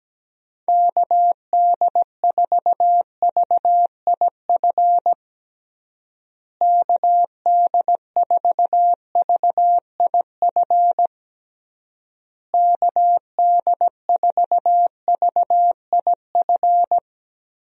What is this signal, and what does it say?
Morse code, heard as a keyed tone, about 17 wpm, 700 Hz. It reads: KD4VIF KD4VIF KD4VIF
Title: USA Call Signs - 6 Characters - Repeated 3x - 17wpm